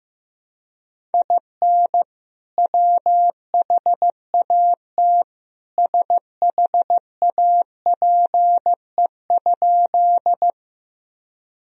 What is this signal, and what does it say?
Morse code, heard as a keyed tone, about 15 wpm, 700 Hz. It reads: IN WHAT SHAPE?
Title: Sentences from Top 1000 Words - Review Part 3 - 15wpm